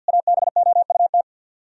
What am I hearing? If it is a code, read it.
Morse code, heard as a keyed tone, about 61 wpm, 700 Hz. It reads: ABOUT
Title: Speed-Racing - Sets of 4 Words 50wpm